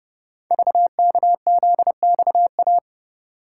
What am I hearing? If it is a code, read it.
Morse code, heard as a keyed tone, about 30 wpm, 700 Hz. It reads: VKZXA
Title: Code Groups - 5 Letters - 30wpm